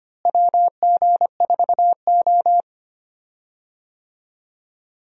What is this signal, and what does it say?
Morse code, heard as a keyed tone, about 25 wpm, 700 Hz. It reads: WG4O
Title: USA Call Signs - 4 Characters - Repeated 2x - 25wpm